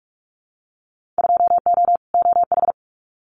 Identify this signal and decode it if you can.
Morse code, heard as a keyed tone, about 46 wpm, 700 Hz. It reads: 2O OH
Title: Winter Field Day Exchanges 45wpm